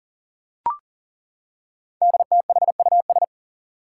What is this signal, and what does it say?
Morse code, heard as a keyed tone, about 42 wpm, 700 Hz. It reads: DTHUS
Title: Five Challenging Letter-Number Combinations 40wpm